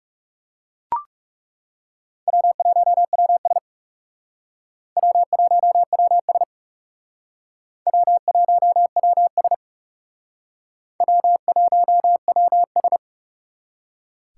Morse code, heard as a keyed tone, about 36 wpm, 700 Hz. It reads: W1WS W1WS W1WS W1WS
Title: Speed Racing - USA 1x2 Call Signs 30wpm